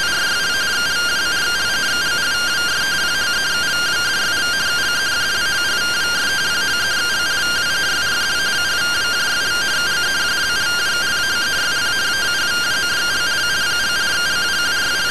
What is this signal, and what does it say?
Two-tone teleprinter signal, about 170 Hz shift.